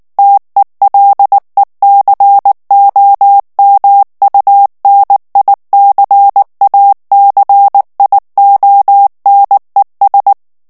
Morse code, heard as a keyed tone, about 19 wpm, 800 Hz. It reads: TELECOMUNICACIONES
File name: morse mp3 / ESPAGNOL TEXTE EN CLAIR 2 VITESSE 18 MOTS MINUTES